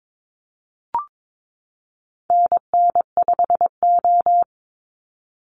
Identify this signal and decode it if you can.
Morse code, heard as a keyed tone, about 22 wpm, 700 Hz. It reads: NN5O
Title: Call Signs Repeated 2x - 22wpm